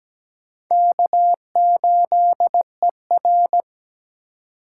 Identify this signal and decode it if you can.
Morse code, heard as a keyed tone, about 17 wpm, 700 Hz. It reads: K8ER
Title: USA Call Signs - 4 Characters - 3x Character Spacing - 17wpm